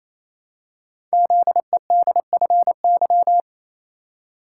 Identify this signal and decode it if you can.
Morse code, heard as a keyed tone, about 27 wpm, 700 Hz. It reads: ZEDFY
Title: Code Groups - 5 Letters - 28wpm